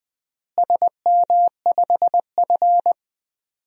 Morse code, heard as a keyed tone, about 20 wpm, 700 Hz. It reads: SM5F
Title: DXCC-100 Call Signs - 4 Characters - Repeated 2x - 20wpm